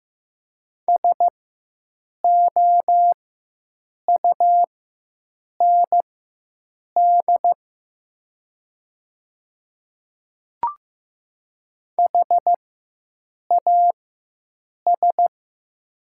Morse code, heard as a keyed tone, about 15 wpm, spaced as for 7 wpm, 700 Hz. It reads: SOUND HAS
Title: 4x Character Spacing - Top 100 Words - 15wpm